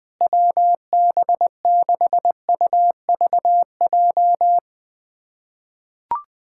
Morse code, heard as a keyed tone, about 21 wpm, 700 Hz. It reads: WB6UVJ
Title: USA Callsign DE Callsign - 6 Characters - 20wpm